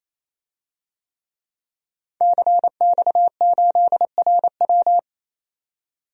Morse code, heard as a keyed tone, about 28 wpm, 700 Hz. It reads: CX8RW
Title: DXCC-100 Call Signs - 5 Characters - 28wpm